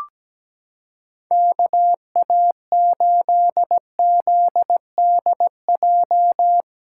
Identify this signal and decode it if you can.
Morse code, heard as a keyed tone, about 17 wpm, 700 Hz. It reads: KA8ZDJ
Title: USA Callsign DE Callsign 17wpm